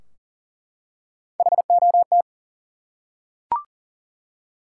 Morse code, heard as a keyed tone, about 39 wpm, 700 Hz. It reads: HOT
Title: Three Letter Words - 40wpm